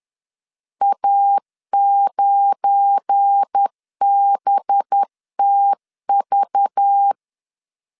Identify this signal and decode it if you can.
Morse code, heard as a keyed tone, about 10 wpm, 800 Hz. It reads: A9BTV
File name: LEZIONI / 7 GF9Z / 7 GF9 10wpm 2747